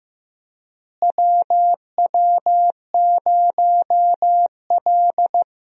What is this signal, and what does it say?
Morse code, heard as a keyed tone, about 15 wpm, 700 Hz. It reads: WW0L
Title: USA Call Signs - 4 Characters - Repeated 2x - 15wpm